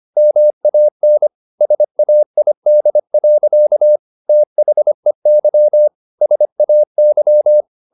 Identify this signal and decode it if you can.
Morse code, heard as a keyed tone, about 25 wpm, 600 Hz. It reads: MAN SAID. THEY SAY
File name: the old man and the sea part 1 25wpm